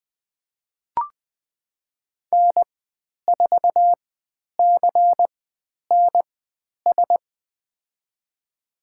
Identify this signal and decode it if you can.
Morse code, heard as a keyed tone, about 20 wpm, spaced as for 10 wpm, 700 Hz. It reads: N4CNS
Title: 095 Lesson 16g - Letter c - Call Signs 20wpm